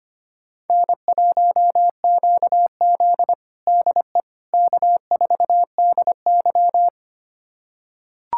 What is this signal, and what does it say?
Morse code, heard as a keyed tone, about 25 wpm, 700 Hz. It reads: N1QZ DE K4DY
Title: USA Callsign DE Callsign - 4 Characters - 25wpm